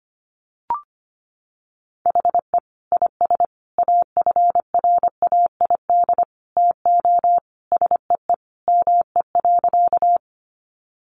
Morse code, heard as a keyed tone, about 25 wpm, 700 Hz. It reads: HE IS AFRAID TO SEE ME.
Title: Sentences from Top 1500 Words 25wpm